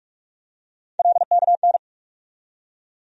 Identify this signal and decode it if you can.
Morse code, heard as a keyed tone, about 46 wpm, 700 Hz. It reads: LKN
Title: USA Call Sign Postfixes - 3 Letters 45wpm